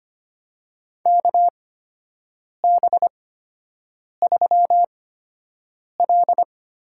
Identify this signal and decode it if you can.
Morse code, heard as a keyed tone, about 25 wpm, spaced as for 7 wpm, 700 Hz. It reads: KB3L
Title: USA Call Signs - 8x Character Spacing - 25wpm